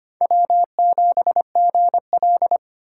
Morse code, heard as a keyed tone, about 32 wpm, 700 Hz. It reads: W7GL